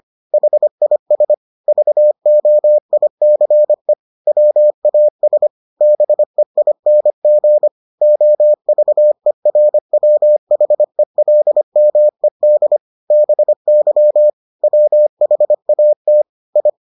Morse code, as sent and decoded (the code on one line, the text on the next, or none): .... .. ... / ...- --- .. -.-. / .-- .- ... / -... .. -. --. / --- ...- . .-. .-- .... .-.. -- . -.. / -... -.-- / .-- .... .- - / ..
HIS VOICE WAS BEING OVERWHELMED BY WHAT I